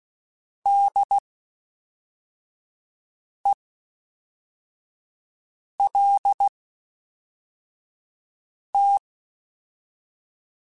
Morse code, heard as a keyed tone, about 16 wpm, spaced as for 4 wpm, 800 Hz. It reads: DELT
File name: (spaced) / FromSMs spaced mp3s / Classes@5